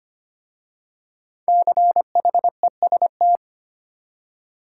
Morse code, heard as a keyed tone, about 25 wpm, 700 Hz. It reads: CHEST
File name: 5 Letter Words - First 3 Letters Given - 25wpm